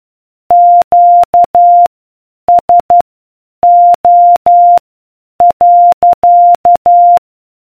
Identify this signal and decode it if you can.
Morse code, heard as a keyed tone, about 12 wpm, spaced as for 8 wpm, 700 Hz. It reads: QSO.